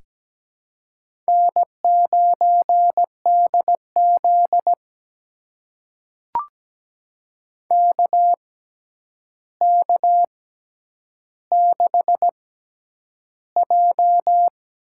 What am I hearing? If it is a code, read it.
Morse code, heard as a keyed tone, about 17 wpm, spaced as for 9 wpm, 700 Hz. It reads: N9DZ KK6J
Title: USA Call Signs - 6x Character Spacing - 17wpm